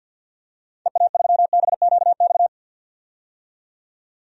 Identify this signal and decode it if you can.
Morse code, heard as a keyed tone, about 47 wpm, 700 Hz. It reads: EA3BQX
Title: DXCC-100 Call Signs - 6 Characters - 50wpm